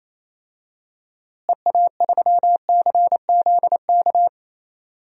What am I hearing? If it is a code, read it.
Morse code, heard as a keyed tone, about 28 wpm, 700 Hz. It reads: EA3CZK
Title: DXCC-100 Call Signs - 6 Characters - Repeated 2x - 28wpm